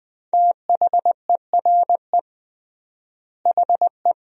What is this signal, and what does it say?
Morse code, heard as a keyed tone, about 21 wpm, 700 Hz. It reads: THERE HE
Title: 3x Word-Spacing - Sentences from Top 100 Words - 20wpm